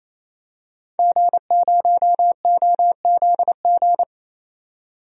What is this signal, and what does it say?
Morse code, heard as a keyed tone, about 28 wpm, 700 Hz. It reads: G0OZG
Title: DXCC-100 Call Signs - 5 Characters - 28wpm